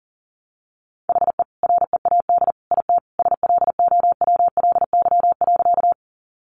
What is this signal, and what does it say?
Morse code, heard as a keyed tone, about 40 wpm, 700 Hz. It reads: HE READ IT SLOWLY.
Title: Sentences from Top 1100 Words 40wpm